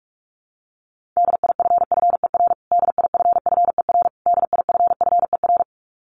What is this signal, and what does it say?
Morse code, heard as a keyed tone, about 44 wpm, 700 Hz. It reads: DIFFER DIFFER DIFFER
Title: Top 500 Words Repeated 45wpm